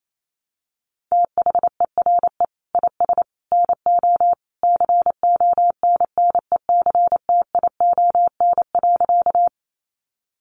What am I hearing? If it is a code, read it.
Morse code, heard as a keyed tone, about 28 wpm, 700 Hz. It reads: THERE IS NO CONNECTION.
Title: Sentences from Top 1300 Words 28wpm